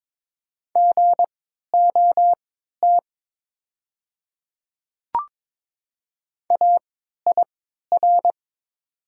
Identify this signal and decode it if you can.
Morse code, heard as a keyed tone, about 22 wpm, spaced as for 12 wpm, 700 Hz. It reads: GOT AIR
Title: 3x Character Spacing - Top 300 Words - 22wpm